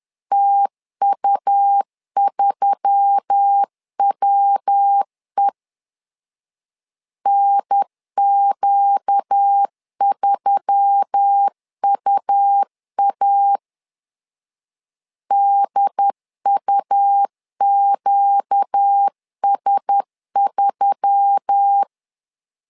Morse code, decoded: TU3WE NQ3UA DUQS3

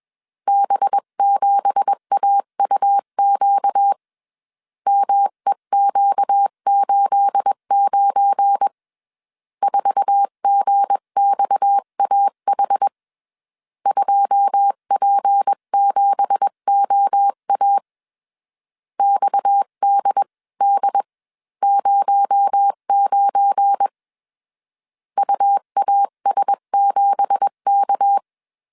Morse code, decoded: B7AUQ MEQ89 4GXAH 2P7OA XDD09 UAS7K

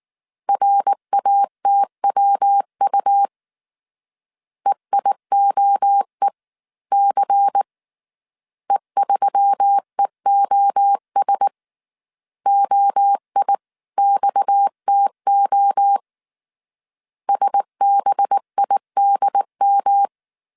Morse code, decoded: RATWU EIOEC E3EOS OIXTO SBIDM

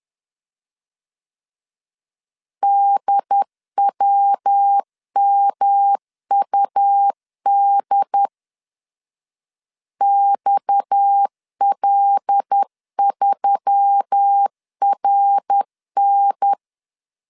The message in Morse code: DWMUD XL3RN

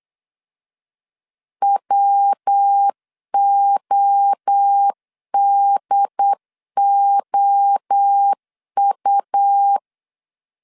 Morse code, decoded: WODOU

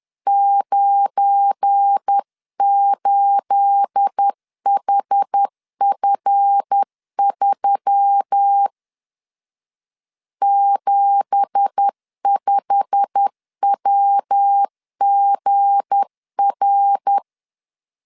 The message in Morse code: 98HF3 75WGR

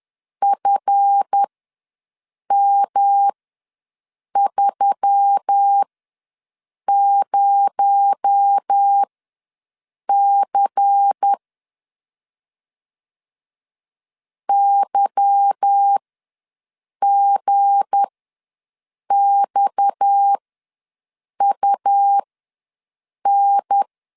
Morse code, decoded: FM30C YGXUN

